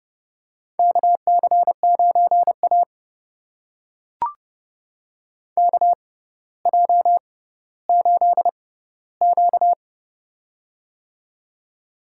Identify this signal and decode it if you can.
Morse code, heard as a keyed tone, about 30 wpm, spaced as for 15 wpm, 700 Hz. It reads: KC9A KJ8Q
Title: USA Call Signs - 4 Characters - 6x Character Spacing - 30wpm